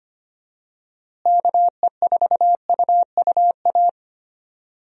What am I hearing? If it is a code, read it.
Morse code, heard as a keyed tone, about 25 wpm, 700 Hz. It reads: KE4UUA
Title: USA Call Signs - 6 Characters - Repeated 2x - 25wpm